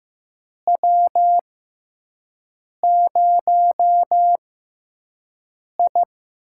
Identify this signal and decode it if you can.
Morse code, heard as a keyed tone, about 15 wpm, spaced as for 5 wpm, 700 Hz. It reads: W0I
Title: USA Call Signs - 4 Characters - 6x Character Spacing - 15wpm